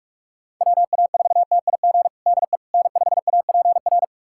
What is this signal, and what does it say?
Morse code, heard as a keyed tone, about 46 wpm, 700 Hz. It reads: WA4TIG DE N5APR
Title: USA Callsign DE Callsign 45wpm